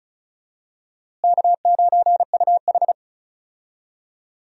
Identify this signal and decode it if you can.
Morse code, heard as a keyed tone, about 34 wpm, 700 Hz. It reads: K9UH
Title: USA Call Signs - 4 Characters - Repeated 2x - 35wpm